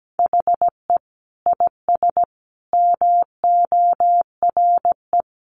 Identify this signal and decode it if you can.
Morse code, heard as a keyed tone, about 17 wpm, 700 Hz. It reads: HE IS MORE
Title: Sentences from Top 1300 Words 17wpm